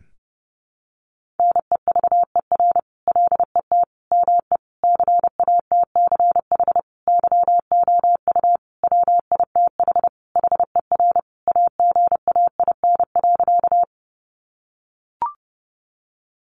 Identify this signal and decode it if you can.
Morse code, heard as a keyed tone, about 30 wpm, 700 Hz. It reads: NEVER LET ME CATCH YOU WITH HER AGAIN.
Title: Sentences from Top 1500 Words 30wpm